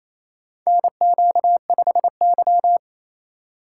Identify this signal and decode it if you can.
Morse code, heard as a keyed tone, about 28 wpm, 700 Hz. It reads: NQ5Y
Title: USA Call Signs - 4 Characters - 2x Character Spacing - 28wpm